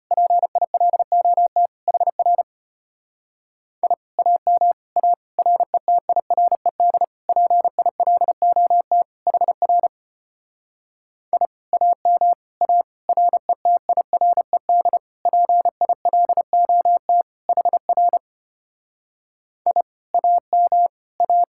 Morse code, heard as a keyed tone, about 29 wpm, 700 Hz. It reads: PILOT HR I AM A RETIRED PILOT HR I AM A RETIRED PILOT HR I AM A
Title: Speed-Racing Common QSO Phrases Encore - 25wpm